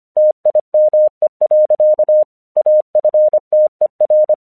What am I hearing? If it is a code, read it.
Morse code, heard as a keyed tone, about 25 wpm, 610 Hz. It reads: TIME. AFTER
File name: 1 English 25wpm / AQC36